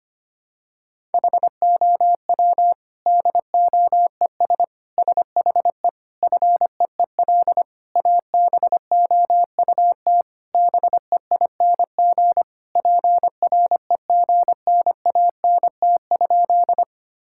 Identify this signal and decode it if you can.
Morse code, heard as a keyed tone, about 25 wpm, 700 Hz. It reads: HOW DOES SHE FEEL ABOUT BEING PREGNANT?
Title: Sentences from Top 1000 Words - Review Part 6 - 25wpm